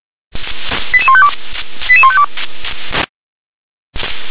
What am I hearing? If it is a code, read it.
Five-tone selective call (ZVEI), heard as quick runs of five tones, tones of about 70 ms. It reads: 88152 88152